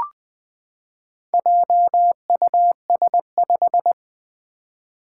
Morse code, heard as a keyed tone, about 20 wpm, 700 Hz. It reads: JUS5
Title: Four Challenging Letter-Number Combinations 20wpm